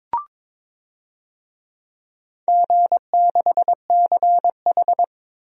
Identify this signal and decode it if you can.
Morse code, heard as a keyed tone, about 22 wpm, 700 Hz. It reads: G6CH
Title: DXCC-100 Call Signs - 4 Characters - Repeated 2x - 22wpm